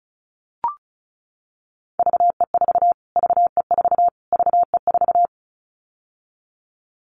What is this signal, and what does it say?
Morse code, heard as a keyed tone, about 34 wpm, 700 Hz. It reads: VE4 VE4 VE4
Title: Callsign Prefixes for Top 25 DXCC - Repeated 3x - 35wpm